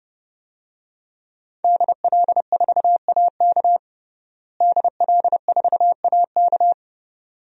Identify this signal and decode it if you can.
Morse code, heard as a keyed tone, about 32 wpm, 700 Hz. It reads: DL4AK DL4AK
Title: DXCC-100 Call Signs - 5 Characters - Repeated 2x - 30wpm